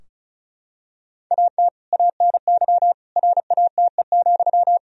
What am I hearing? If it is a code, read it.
Morse code, heard as a keyed tone, about 34 wpm, 700 Hz. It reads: AT ANY RATE,
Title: Sentences from Top 1000 Words - Review Part 4 - 35wpm